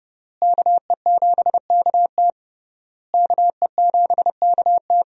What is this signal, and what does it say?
Morse code, heard as a keyed tone, about 30 wpm, 700 Hz. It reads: KE7KT KE7KT
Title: USA Call Signs - 5 Characters - Repeated 3x - 30wpm